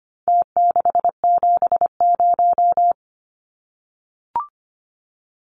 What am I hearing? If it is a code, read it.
Morse code, heard as a keyed tone, about 25 wpm, 700 Hz. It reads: T670
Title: Numbers - Contest Style - 4-Digit 25wpm